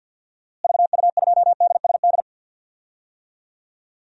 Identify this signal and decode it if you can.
Morse code, heard as a keyed tone, about 47 wpm, 700 Hz. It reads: VU2DSD
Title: DXCC-100 Call Signs - 6 Characters - Repeated 2x - 50wpm